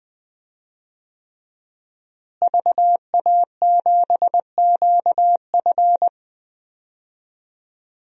Morse code, heard as a keyed tone, about 20 wpm, 700 Hz. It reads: VA7QF
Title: DXCC-100 Call Signs - 5 Characters - Repeated 3x - 20wpm